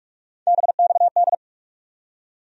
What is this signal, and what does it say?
Morse code, heard as a keyed tone, about 46 wpm, 700 Hz. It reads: BXD
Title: USA Call Sign Postfixes - 3 Letters 45wpm